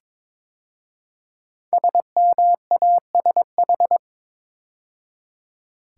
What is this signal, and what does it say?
Morse code, heard as a keyed tone, about 22 wpm, 690 Hz. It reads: SMASH
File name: 5 Letter Words - First 3 Letters Given - 22wpm